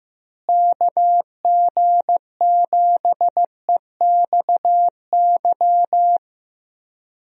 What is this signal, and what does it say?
Morse code, heard as a keyed tone, about 15 wpm, 700 Hz. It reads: KG7EXY
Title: USA Call Signs - 6 Characters - Repeated 2x - 15wpm